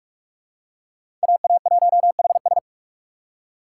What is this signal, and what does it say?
Morse code, heard as a keyed tone, about 41 wpm, 700 Hz. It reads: AA1HS